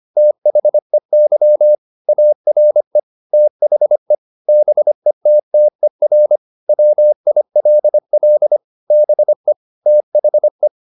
Morse code, heard as a keyed tone, about 25 wpm, 600 Hz. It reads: THEY ARE THE BETTER WILL BE THE
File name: SS311 25